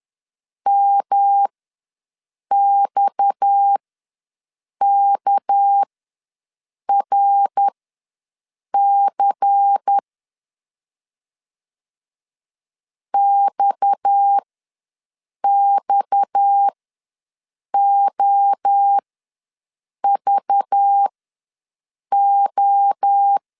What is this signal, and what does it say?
Morse code, heard as a keyed tone, about 10 wpm, spaced as for 6 wpm, 800 Hz. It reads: MXKRC XXOVO
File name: LEZIONI / 6 PKX0 / 6 PKX ws 10wpm 3706